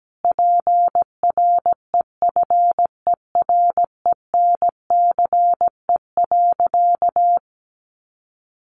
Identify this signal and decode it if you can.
Morse code, heard as a keyed tone, about 17 wpm, 700 Hz. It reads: PREFERENCE.
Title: Sentences from Top 1500 Words 17wpm